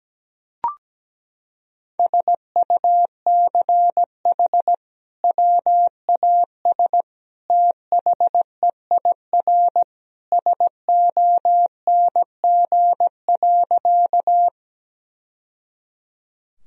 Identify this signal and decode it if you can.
Morse code, heard as a keyed tone, about 17 wpm, 700 Hz. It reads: SUCH WAS THEIR SONG.
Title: Sentences from Top 1000 Words - Review Part 2 - 17wpm